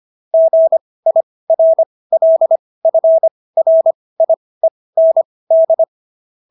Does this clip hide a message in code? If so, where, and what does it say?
Yes: Morse code from 0.3 to 5.8 s, heard as a keyed tone, about 25 wpm, spaced as for 18 wpm, 650 Hz. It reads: GIRLFRIEND